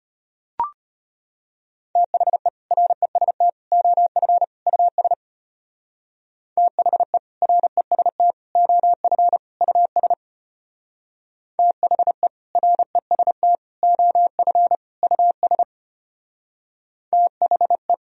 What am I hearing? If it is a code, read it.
Morse code, heard as a keyed tone, about 33 wpm, 700 Hz. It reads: THE REST OF US THE REST OF US THE REST OF US THE